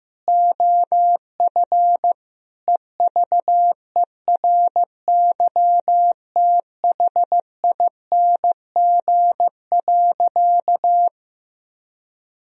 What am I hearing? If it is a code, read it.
Morse code, heard as a keyed tone, about 16 wpm, 700 Hz. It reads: OF EVERYTHING.
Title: Sentences from Top 1000 Words - Review Part 2 - 15wpm